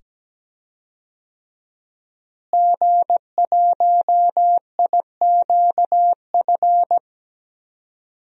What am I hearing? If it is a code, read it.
Morse code, heard as a keyed tone, about 17 wpm, 700 Hz. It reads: G1IQF
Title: DXCC-100 Call Signs - 5 Characters - Repeated 2x - 17wpm